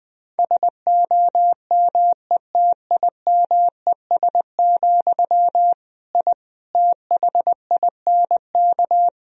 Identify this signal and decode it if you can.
Morse code, heard as a keyed tone, about 21 wpm, 700 Hz. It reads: SOMETIMES, I THINK
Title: Sentences from Top 1000 Words - Review Part 3 - 20wpm